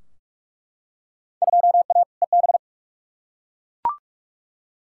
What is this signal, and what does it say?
Morse code, heard as a keyed tone, about 42 wpm, 700 Hz. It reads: 2A EB
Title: Field Day Exchanges - 45wpm